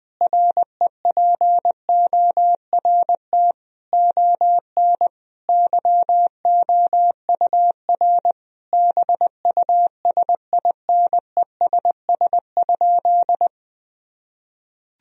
Morse code, heard as a keyed tone, about 20 wpm, 700 Hz. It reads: REPORT ON YOUR BUSINESS?
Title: Sentences from Top 1000 Words - Review Part 4 - 20wpm